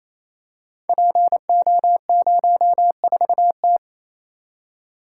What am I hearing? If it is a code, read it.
Morse code, heard as a keyed tone, about 28 wpm, 700 Hz. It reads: PO04T
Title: Code Groups - 5 Letter:Numbers - 28wpm